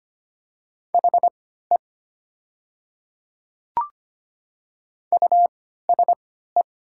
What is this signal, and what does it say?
Morse code, heard as a keyed tone, about 25 wpm, spaced as for 14 wpm, 700 Hz. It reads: HE USE
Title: 3x Character Spacing - Top 100 Words - 25wpm